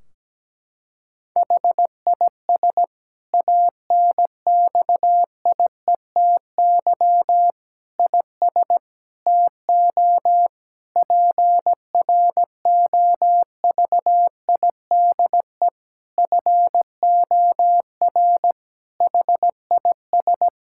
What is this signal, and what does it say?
Morse code, heard as a keyed tone, about 17 wpm, 700 Hz. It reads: HIS ANXIETY IS TO PROVIDE FOR HIS
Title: Sentences from Top 1600 Words 17wpm